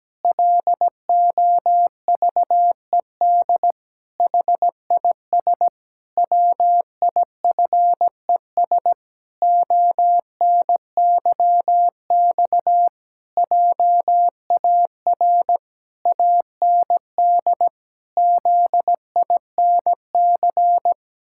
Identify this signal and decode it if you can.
Morse code, heard as a keyed tone, about 17 wpm, 700 Hz. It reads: LOVED HIS WIFES ONYX JAR AND ZINC
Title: Pangram Challenge v2 - 17wpm